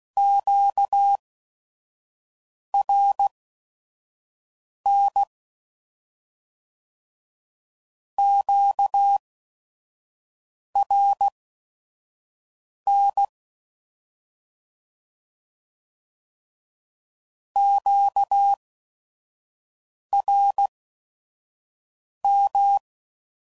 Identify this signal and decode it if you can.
Morse code, heard as a keyed tone, about 16 wpm, spaced as for 5 wpm, 800 Hz. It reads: QRN QRN QRM
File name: CW Code Groups @5WPM / Q Calls / Q Calls Twice